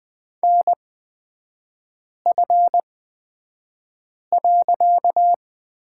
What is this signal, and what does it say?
Morse code, heard as a keyed tone, about 20 wpm, spaced as for 5 wpm, 700 Hz. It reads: NF.
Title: 255 Lesson 41e - Period - Three Characters 20wpm